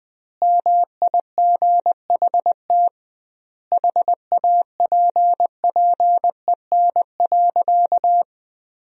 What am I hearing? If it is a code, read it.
Morse code, heard as a keyed tone, about 20 wpm, 700 Hz. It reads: MIGHT HAPPEN.
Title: 2x Word-Spacing - Sentences from Top 400 Words - 20wpm